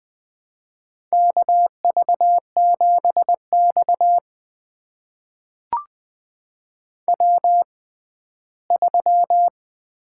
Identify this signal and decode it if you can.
Morse code, heard as a keyed tone, about 20 wpm, spaced as for 14 wpm, 700 Hz. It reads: KV7X W3
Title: USA Call Signs - 4 Characters - 6x Character Spacing - 20wpm